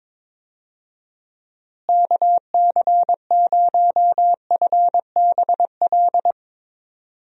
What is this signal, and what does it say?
Morse code, heard as a keyed tone, about 22 wpm, 700 Hz. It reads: KC0FBL
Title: USA Call Signs - 6 Characters - Repeated 2x - 22wpm